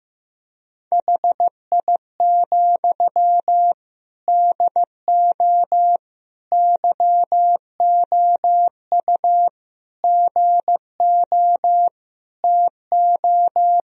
Morse code, heard as a keyed tone, about 15 wpm, 700 Hz. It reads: HI, DO YOU GO TO
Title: Sentences from Top 1000 Words - Review Part 3 - 15wpm